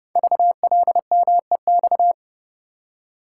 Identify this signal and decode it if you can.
Morse code, heard as a keyed tone, about 30 wpm, 700 Hz. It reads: VLMEX
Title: Code Groups - 5 Letters - 30wpm